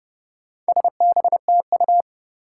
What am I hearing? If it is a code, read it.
Morse code, heard as a keyed tone, about 30 wpm, 700 Hz. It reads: SBTU
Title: Four Challenging Letter-Number Combinations 30wpm